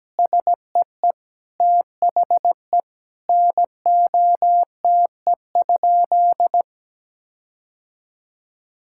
Morse code, heard as a keyed tone, about 17 wpm, 700 Hz. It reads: SEE THE NOTE?
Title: Sentences from Top 1600 Words 17wpm